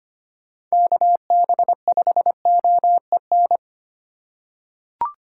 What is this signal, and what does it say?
Morse code, heard as a keyed tone, about 25 wpm, 700 Hz. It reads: KB5OEN
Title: USA Call Signs - 6 Characters - 2x Character Spacing - 25wpm